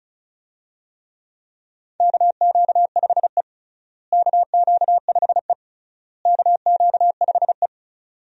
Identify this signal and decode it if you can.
Morse code, heard as a keyed tone, about 36 wpm, 700 Hz. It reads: KQ5E KQ5E KQ5E
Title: USA Call Signs - 4 Characters - Repeated 3x - 35wpm